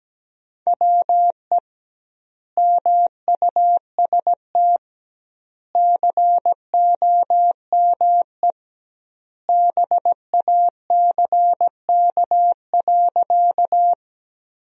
Morse code, heard as a keyed tone, about 17 wpm, 700 Hz. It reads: WE MUST COME BACK.
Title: 2x Word-Spacing - Sentences from Top 200 Words - 17wpm